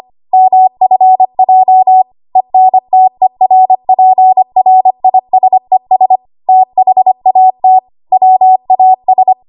Morse code, heard as a keyed tone, about 25 wpm, 750 Hz. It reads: MFJ ENTERPRISES THAT WAS
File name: Morse / 25WPM